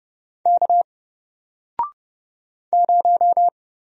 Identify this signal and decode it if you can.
Morse code, heard as a keyed tone, about 30 wpm, 700 Hz. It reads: K 0
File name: Single-Letter-Number-30wpm